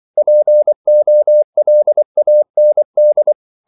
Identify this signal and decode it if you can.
Morse code, heard as a keyed tone, about 24 wpm, 600 Hz. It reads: POLAND
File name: CW / Practice / NationalCapitals24